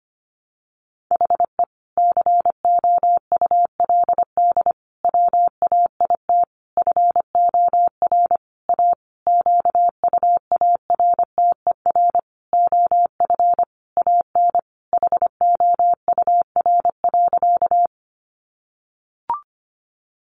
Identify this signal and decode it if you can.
Morse code, heard as a keyed tone, about 25 wpm, 700 Hz. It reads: HE COULD WAIT FOR A QUARTER OF AN HOUR.
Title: Sentences from Top 1300 Words 25wpm